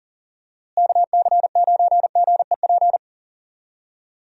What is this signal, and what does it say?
Morse code, heard as a keyed tone, about 40 wpm, 700 Hz. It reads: KC9GEP
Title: USA Call Signs - 6 Characters - Repeated 2x - 40wpm